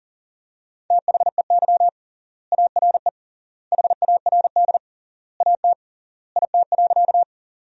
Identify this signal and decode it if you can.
Morse code, heard as a keyed tone, about 40 wpm, 700 Hz. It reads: THEY ARE HARD AT IT.
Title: 3x Word-Spacing - Sentences from Top 300 Words - 40wpm